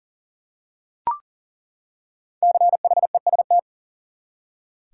Morse code, heard as a keyed tone, about 40 wpm, 700 Hz. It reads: CHEST
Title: Five Letter Words - 40wpm